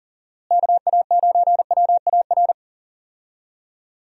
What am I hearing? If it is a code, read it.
Morse code, heard as a keyed tone, about 39 wpm, 700 Hz. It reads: KA9WAR
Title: USA Call Signs - 6 Characters - Repeated 2x - 40wpm